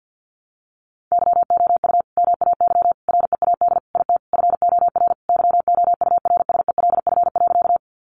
Morse code, heard as a keyed tone, about 50 wpm, 700 Hz. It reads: YOU MAY READ IT FOR YOURSELF.
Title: Sentences from Top 600 Words 50wpm